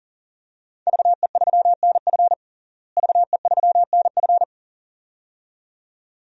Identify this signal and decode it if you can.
Morse code, heard as a keyed tone, about 40 wpm, 700 Hz. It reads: VE3NF VE3NF
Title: DXCC-100 Call Signs - 5 Characters - Repeated 2x - 40wpm